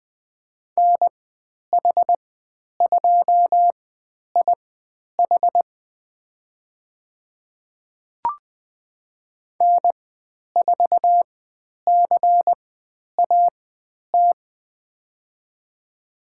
Morse code, heard as a keyed tone, about 20 wpm, spaced as for 10 wpm, 700 Hz. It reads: NH2IH N4CAT
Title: 109 Lesson 18g - Letter m - Call Signs 20wpm